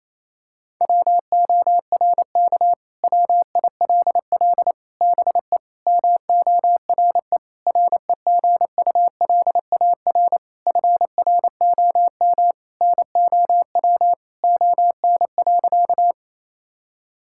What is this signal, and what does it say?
Morse code, heard as a keyed tone, about 28 wpm, 700 Hz. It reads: WORK WILL BE MORE REGULAR FROM NOW ON.
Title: Sentences from Top 1000 Words - Review Part 6 - 28wpm